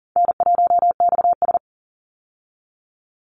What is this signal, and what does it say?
Morse code, heard as a keyed tone, about 40 wpm, 700 Hz. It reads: N1XS
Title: Easy Call Signs Repeated 2x - 40wpm